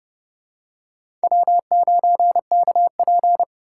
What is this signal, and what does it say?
Morse code, heard as a keyed tone, about 30 wpm, 700 Hz. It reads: W9KP
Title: USA Call Signs - 4 Characters - 3x Character Spacing - 30wpm